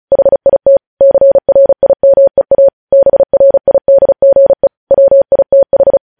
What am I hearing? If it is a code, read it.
Morse code, heard as a keyed tone, about 36 wpm, 550 Hz. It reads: HIT CRIMEA BRIDGE WITH
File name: podcast / News headlines-1x-35x35